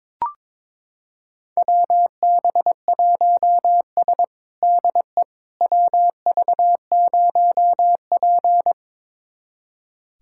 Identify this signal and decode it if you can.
Morse code, heard as a keyed tone, about 22 wpm, 700 Hz. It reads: WB1S DE WV0P
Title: USA Callsign DE Callsign - 4 Characters - 22wpm